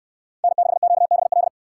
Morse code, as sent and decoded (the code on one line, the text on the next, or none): -. -.... --.- -... .-..
N6QBL